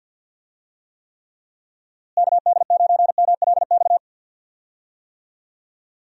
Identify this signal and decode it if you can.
Morse code, heard as a keyed tone, about 50 wpm, 700 Hz. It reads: KD9MLX
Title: USA Call Signs - 6 Characters - Repeated 3x - 50wpm